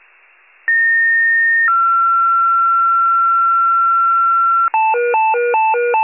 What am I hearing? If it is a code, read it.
Two-tone page, heard as a one-second tone, then a three-second tone, about 1850 then 1360 Hz. It is heard from 0.7 to 4.7 s.
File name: pager / UNKNOWN / wcalert